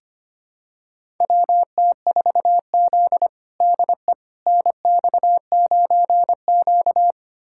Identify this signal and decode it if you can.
Morse code, heard as a keyed tone, about 25 wpm, 700 Hz. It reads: WT4Z DE NX9Q
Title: USA Callsign DE Callsign - 4 Characters - 25wpm